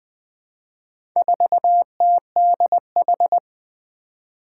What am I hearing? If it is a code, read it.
Morse code, heard as a keyed tone, about 21 wpm, 700 Hz. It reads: 4TDH
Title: Four Challenging Letter-Number Combinations 20wpm